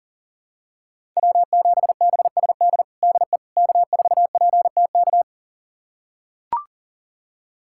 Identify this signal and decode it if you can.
Morse code, heard as a keyed tone, about 40 wpm, 700 Hz. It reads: W7BSD DE K4PTK
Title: USA Callsign DE Callsign - 5 Characters - 40wpm